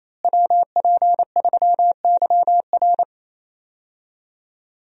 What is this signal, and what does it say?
Morse code, heard as a keyed tone, about 28 wpm, 700 Hz. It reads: WP3YR
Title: USA Call Signs - 5 Characters - Repeated 2x - 28wpm